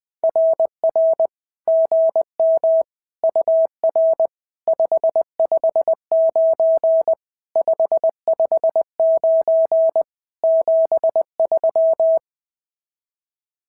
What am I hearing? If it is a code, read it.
Morse code, heard as a keyed tone, about 19 wpm, 650 Hz. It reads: RR GM UR 559 559 73
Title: SOTA QSO Practice 20wpm